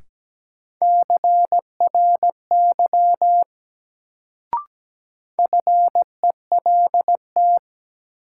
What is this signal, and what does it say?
Morse code, heard as a keyed tone, about 17 wpm, 700 Hz. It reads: CRY FELT
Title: Sets of 5 Words Encore - New Word Drill - 17wpm